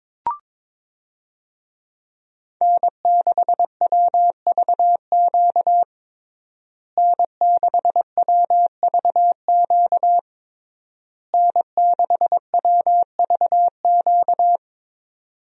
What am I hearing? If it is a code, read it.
Morse code, heard as a keyed tone, about 21 wpm, 700 Hz. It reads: N6WVQ N6WVQ N6WVQ
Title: USA Call Signs - 5 Characters - Repeated 3x - 22wpm